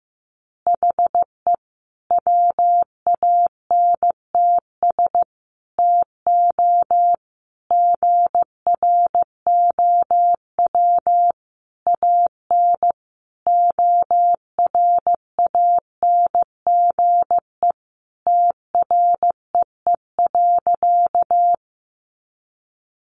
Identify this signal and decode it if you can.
Morse code, heard as a keyed tone, about 15 wpm, 700 Hz. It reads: HE WANTS TO GROW AN ORANGE TREE.
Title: Easy Phrases 15wpm